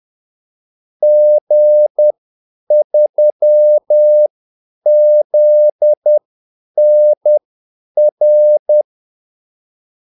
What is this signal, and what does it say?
Morse code, heard as a keyed tone, about 10 wpm, spaced as for 8 wpm, 600 Hz. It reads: G3ZNR
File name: Callsigns-10-8wpm